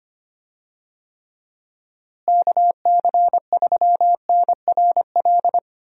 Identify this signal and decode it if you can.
Morse code, heard as a keyed tone, about 25 wpm, 700 Hz. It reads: KC3NRL
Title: USA Call Signs - 6 Characters - Repeated 3x - 25wpm